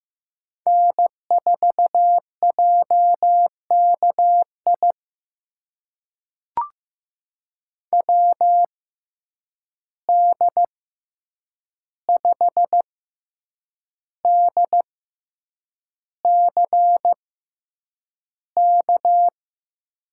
Morse code, heard as a keyed tone, about 15 wpm, spaced as for 7 wpm, 700 Hz. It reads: N4JKI WD5DCK
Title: USA Call Signs - 6x Character Spacing - 15wpm